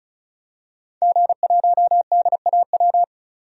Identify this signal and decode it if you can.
Morse code, heard as a keyed tone, about 36 wpm, 700 Hz. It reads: G1DAW
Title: DXCC-100 Call Signs - 5 Characters - 35wpm